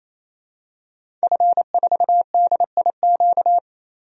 Morse code, heard as a keyed tone, about 28 wpm, 700 Hz. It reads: F4DIQ
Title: DXCC-100 Call Signs - 5 Characters - Repeated 3x - 28wpm